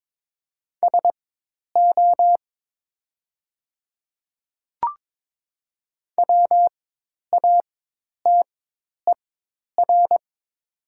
Morse code, heard as a keyed tone, about 22 wpm, spaced as for 10 wpm, 700 Hz. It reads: SO WATER
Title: 4x Character Spacing - Top 100 Words - 22wpm